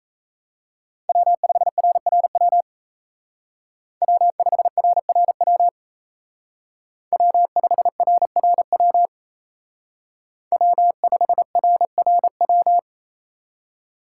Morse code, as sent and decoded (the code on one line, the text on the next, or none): .-- ..... .-. .-. .-- / .-- ..... .-. .-. .-- / .-- ..... .-. .-. .-- / .-- ..... .-. .-. .--
W5RRW W5RRW W5RRW W5RRW